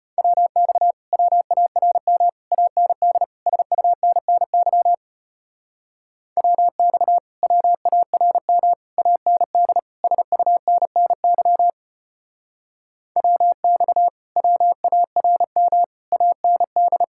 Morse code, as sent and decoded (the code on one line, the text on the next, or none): .-- -..- / .-- .- .-. -- / .- -. -.. / ... ..- -. -. -.-- / .-- -..- / .-- .- .-. -- / .- -. -.. / ... ..- -. -. -.-- / .-- -..- / .-- .- .-. -- / .- -. -..
WX WARM AND SUNNY WX WARM AND SUNNY WX WARM AND